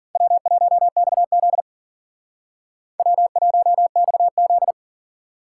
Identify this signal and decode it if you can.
Morse code, heard as a keyed tone, about 51 wpm, 700 Hz. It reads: W1XZ W1XZ